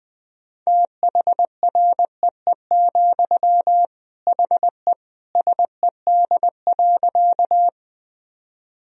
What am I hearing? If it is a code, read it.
Morse code, heard as a keyed tone, about 19 wpm, 700 Hz. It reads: THREE, HE SED.
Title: Sentences from Top 200 Words and Abbreviations 20wpm